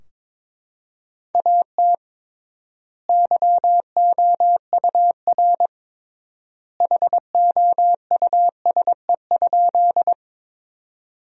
Morse code, heard as a keyed tone, about 21 wpm, 700 Hz. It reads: AT YOUR HOUSE?
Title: 3x Word-Spacing - Sentences from Top 200 Words - 22wpm